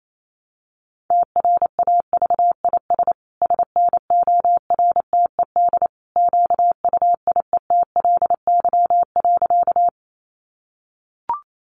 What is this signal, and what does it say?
Morse code, heard as a keyed tone, about 28 wpm, 700 Hz. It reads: TRAVIS SNORTED QUIETLY.
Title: A Sound Of Thunder 28wpm